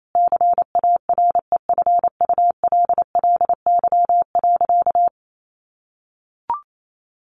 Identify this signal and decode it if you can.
Morse code, heard as a keyed tone, about 28 wpm, 700 Hz. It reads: CAREFULLY.
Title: Sentences from Top 1300 Words 28wpm